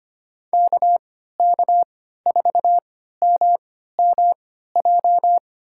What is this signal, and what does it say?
Morse code, heard as a keyed tone, about 25 wpm, spaced as for 14 wpm, 700 Hz. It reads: KK4MMJ
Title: USA Call Signs - 6 Characters - 3x Character Spacing - 25wpm